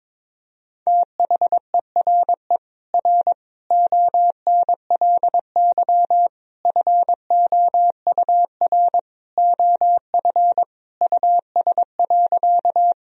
Morse code, heard as a keyed tone, about 21 wpm, 700 Hz. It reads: THERE R ONLY FOUR OF US.
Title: Sentences from Top 300 Words and Abbreviations 22wpm